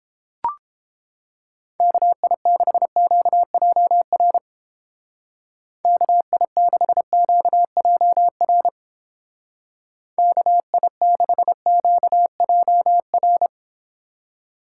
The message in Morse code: KI6QJR KI6QJR KI6QJR

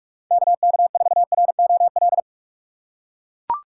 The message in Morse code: KK4ROL

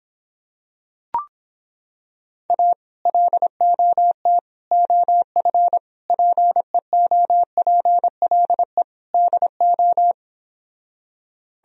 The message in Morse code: A LOT OF PEOPLE DO

